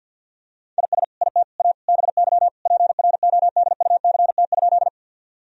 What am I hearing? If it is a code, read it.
Morse code, heard as a keyed tone, about 48 wpm, 700 Hz. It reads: IS IT A BY PRODUCT?
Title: Sentences from Top 1000 Words - Review Part 2 - 50wpm